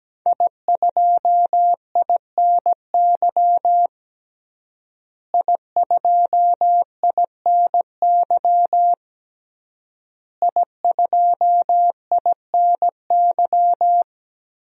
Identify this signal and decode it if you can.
Morse code, heard as a keyed tone, about 17 wpm, 700 Hz. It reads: I2INY I2INY I2INY
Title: DXCC-100 Call Signs - 5 Characters - Repeated 3x - 17wpm